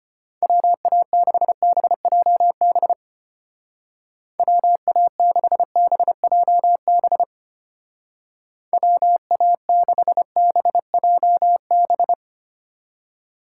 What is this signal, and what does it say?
Morse code, heard as a keyed tone, about 30 wpm, 700 Hz. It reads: WA6BJB WA6BJB WA6BJB
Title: Speed Racing - USA Call Signs 25wpm